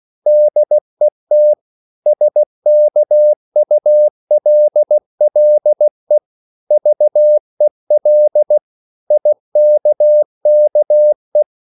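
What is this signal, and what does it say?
Morse code, heard as a keyed tone, about 16 wpm, 600 Hz. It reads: DET SKULLE VEL IKKE
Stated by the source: Slow (QRS) Morse code at 16 wpm, from CW med Gnister.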